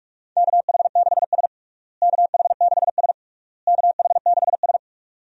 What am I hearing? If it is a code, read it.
Morse code, heard as a keyed tone, about 41 wpm, 700 Hz. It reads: KH6S KH6S KH6S
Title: DXCC-100 Call Signs - 4 Characters - Repeated 3x - 45wpm